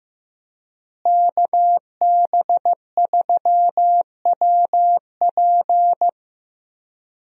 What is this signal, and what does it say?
Morse code, heard as a keyed tone, about 15 wpm, 700 Hz. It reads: KB3WP
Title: USA Call Signs - 5 Characters - Repeated 2x - 15wpm